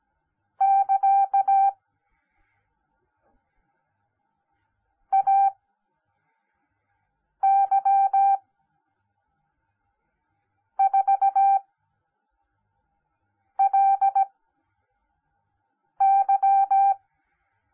Morse code, heard as a keyed tone, about 17 wpm, spaced as for 4 wpm, 780 Hz. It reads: <KA> AY 4LY